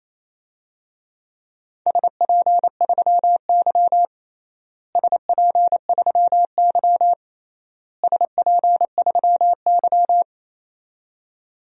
Morse code, heard as a keyed tone, about 28 wpm, 700 Hz. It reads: SP3Y SP3Y SP3Y
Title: DXCC-100 Call Signs - 4 Characters - Repeated 3x - 28wpm